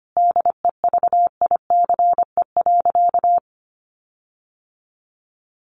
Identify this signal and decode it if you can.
Morse code, heard as a keyed tone, about 25 wpm, 700 Hz. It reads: DEVICE.
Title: Sentences from Top 1500 Words 25wpm